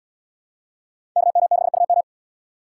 Morse code, heard as a keyed tone, about 78 wpm, 690 Hz. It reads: KM6UM